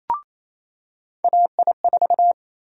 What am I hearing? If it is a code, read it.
Morse code, heard as a keyed tone, about 28 wpm, 700 Hz. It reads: AI4
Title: USA Call Sign Prefixes - 28wpm